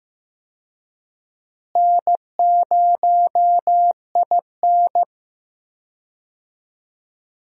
Morse code, heard as a keyed tone, about 15 wpm, 700 Hz. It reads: N0IN
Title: USA Call Signs - 4 Characters - Repeated 2x - 15wpm